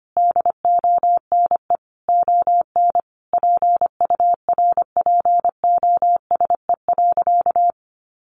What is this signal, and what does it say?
Morse code, heard as a keyed tone, about 25 wpm, 700 Hz. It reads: DONE ON PURPOSE.
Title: Sentences from Top 800 Words 25wpm